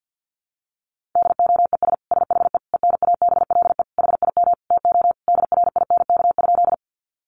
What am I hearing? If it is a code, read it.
Morse code, heard as a keyed tone, about 50 wpm, 700 Hz. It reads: DOES SHE ENABLE HIM TO DRINK?
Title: Sentences from Top 1500 Words 50wpm